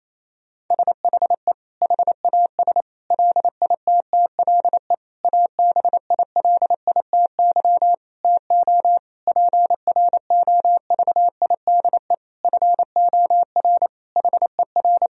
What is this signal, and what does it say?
Morse code, heard as a keyed tone, about 28 wpm, 700 Hz. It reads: SHE HAS LITTLE ABILITY TO PROVIDE FOR HER
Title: Sentences from Top 1000 Words - Review Part 3 - 28wpm